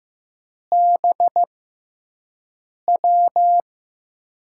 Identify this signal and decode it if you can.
Morse code, heard as a keyed tone, about 15 wpm, spaced as for 5 wpm, 700 Hz. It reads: BW
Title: USA Call Signs - 4 Characters - 6x Character Spacing - 15wpm